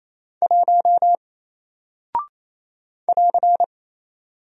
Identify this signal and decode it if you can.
Morse code, heard as a keyed tone, about 27 wpm, 700 Hz. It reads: 1 <AR>
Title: Single Character (Letter:Number:Punctuation:Prosign) Alt 28wpm